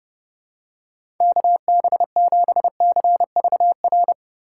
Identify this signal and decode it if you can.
Morse code, heard as a keyed tone, about 30 wpm, 700 Hz. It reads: KB7CVR
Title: USA Call Signs - 6 Characters - 4x Character Spacing - 30wpm